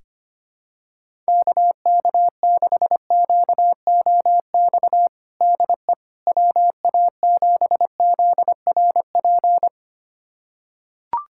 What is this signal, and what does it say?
Morse code, heard as a keyed tone, about 25 wpm, 700 Hz. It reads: KK6QOX DE WA7ZRP
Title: USA Callsign DE Callsign - 6 Characters - 25wpm